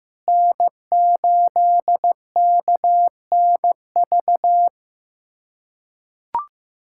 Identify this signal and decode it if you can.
Morse code, heard as a keyed tone, about 15 wpm, 700 Hz. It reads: N8KNV
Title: USA Callsign DE Callsign - 5 Characters - 15wpm